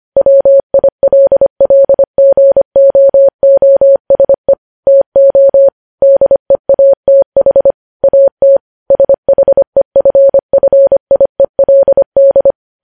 Morse code, heard as a keyed tone, about 25 wpm, 550 Hz. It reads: WILLGOOSE TO DEATH AT SHEFFIELD
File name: podcast / News headlines-1x-25x25